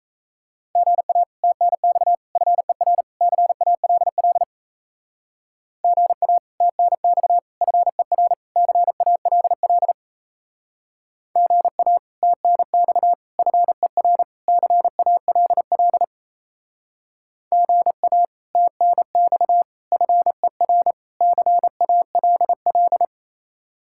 Morse code, heard as a keyed tone, about 36 wpm, 700 Hz. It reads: GA TNX FER CALL GA TNX FER CALL GA TNX FER CALL GA TNX FER CALL